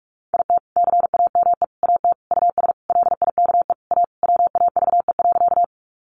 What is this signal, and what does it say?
Morse code, heard as a keyed tone, about 44 wpm, 700 Hz. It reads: IT CAME AT US LIKE A WAVE.
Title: Sentences from Top 1400 Words 45wpm